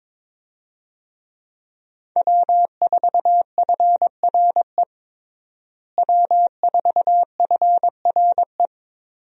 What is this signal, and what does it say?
Morse code, heard as a keyed tone, about 22 wpm, 700 Hz. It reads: W4FRE W4FRE
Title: USA Call Signs - 5 Characters - Repeated 3x - 22wpm